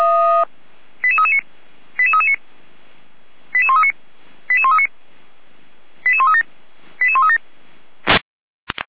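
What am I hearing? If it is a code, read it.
Five-tone selective call (ZVEI), heard as quick runs of five tones, tones of about 80 ms. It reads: 88339 88339 88139 88139 88137 88137
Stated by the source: ZVEI five-tone selective call 88190 received by SDR.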